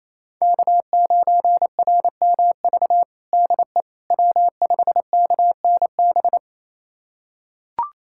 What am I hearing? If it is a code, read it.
Morse code, heard as a keyed tone, about 28 wpm, 700 Hz. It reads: K9RMV DE W5KNB
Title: USA Callsign DE Callsign - 5 Characters - 28wpm